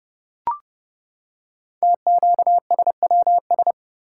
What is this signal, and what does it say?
Morse code, heard as a keyed tone, about 28 wpm, 690 Hz. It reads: TQSWS
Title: Code Groups - 5 Letters - 30wpm